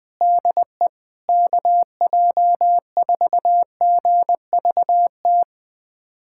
Morse code, heard as a keyed tone, about 20 wpm, 700 Hz. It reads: DE KJ4GVT
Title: USA Callsign DE Callsign - 6 Characters - 20wpm